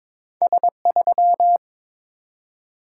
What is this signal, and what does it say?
Morse code, heard as a keyed tone, about 23 wpm, 700 Hz. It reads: S3
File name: practice / 22wpm / Two Letter-Number Combinations - 22wpm